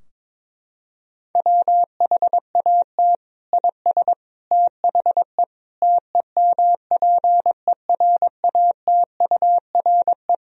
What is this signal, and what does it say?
Morse code, heard as a keyed tone, about 22 wpm, 700 Hz. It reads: WHAT IS THE TEMPERATURE
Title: Sentences from Top 1000 Words - Review Part 4 - 22wpm